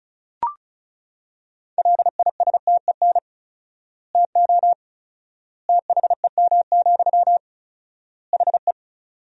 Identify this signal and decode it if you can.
Morse code, heard as a keyed tone, about 36 wpm, 700 Hz. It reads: LISTEN TO THEM, HE
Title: 4x Word-Spacing - Sentences from Top 400 Words - 35wpm